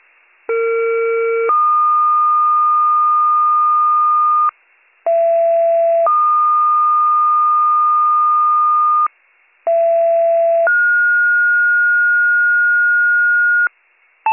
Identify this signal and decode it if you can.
Two-tone page, heard as a one-second tone, then a three-second tone, about 460 then 1220 Hz, about 670 then 1220 Hz, about 670 then 1430 Hz. It is heard from 0.5 to 13.7 s.